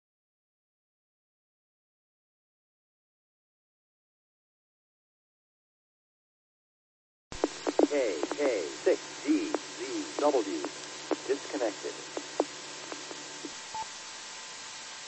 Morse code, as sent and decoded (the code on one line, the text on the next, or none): none